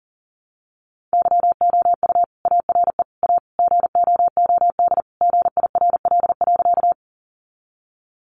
Morse code, heard as a keyed tone, about 40 wpm, 700 Hz. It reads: YOU ARE A GOOD GIRL.
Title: Sentences from Top 400 Words 40wpm